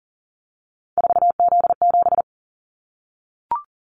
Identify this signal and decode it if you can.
Morse code, heard as a keyed tone, about 38 wpm, 700 Hz. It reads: 4Z7
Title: Callsign Prefixes for Top 50 DXCC - 40wpm